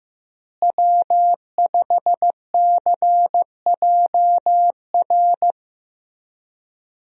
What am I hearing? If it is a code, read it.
Morse code, heard as a keyed tone, about 15 wpm, 700 Hz. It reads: W5CJR
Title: USA Call Signs - 5 Characters - Repeated 3x - 15wpm